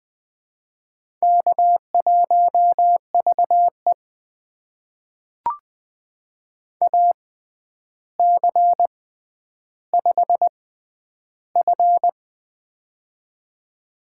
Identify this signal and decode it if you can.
Morse code, heard as a keyed tone, about 20 wpm, spaced as for 10 wpm, 700 Hz. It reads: K1VE AC5F
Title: USA Call Signs - 4 Characters - 6x Character Spacing - 20wpm